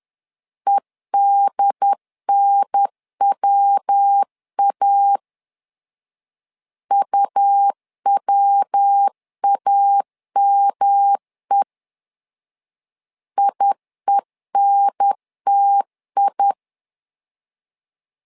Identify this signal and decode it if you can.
Morse code, heard as a keyed tone, about 10 wpm, 800 Hz. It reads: EDNWA UWAME IENTI
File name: LEZIONI / 3 MWRO / 3 MW 10wpm 1959